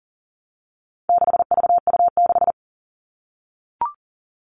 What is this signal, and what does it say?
Morse code, heard as a keyed tone, about 40 wpm, 700 Hz. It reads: 6VU6
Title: All Cut Numbers - 4-Digit 40wpm